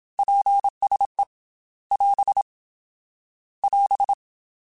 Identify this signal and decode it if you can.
Morse code, heard as a keyed tone, about 27 wpm, 800 Hz. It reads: PSE <AS> <AS>
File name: QSOs / QSOs-mp3s / WA7QOX 2@25